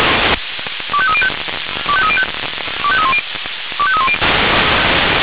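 Five-tone selective call (ZVEI), heard as quick runs of five tones, tones of about 80 ms. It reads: 25305 25305 25310 25310
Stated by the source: ZVEI five-tone selective call received on SDR.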